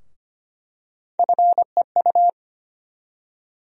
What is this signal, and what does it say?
Morse code, heard as a keyed tone, about 25 wpm, 700 Hz. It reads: FEU
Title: USA Call Sign Postfixes - 3 Letters 25wpm